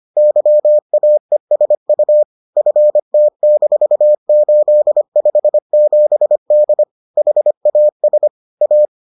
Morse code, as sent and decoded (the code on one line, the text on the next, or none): -.-- .- . ... ..- / ..-. - -....- ---.. ..... --... -.. / .... .- ... / .-
YAESU FT-857D HAS A